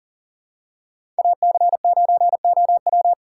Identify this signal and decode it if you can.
Morse code, heard as a keyed tone, about 40 wpm, 700 Hz. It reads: AC9OW
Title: USA Call Signs - 5 Characters - 4x Character Spacing - 40wpm